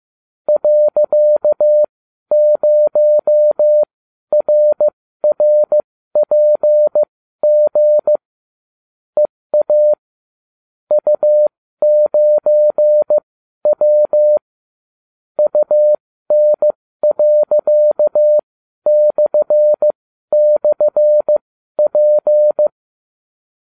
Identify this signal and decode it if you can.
Morse code, heard as a keyed tone, about 15 wpm, spaced as for 12 wpm, 600 Hz. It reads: .0RRPG EA U9W UN.//P